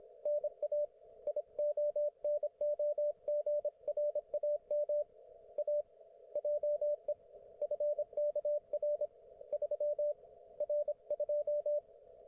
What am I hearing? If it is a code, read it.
Morse code, heard as a keyed tone, about 25 wpm, 590 Hz, with radio noise. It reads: NA IONOGRAM A JE FKR 3R2
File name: Archiv / CUC194 / ok1fhi cul194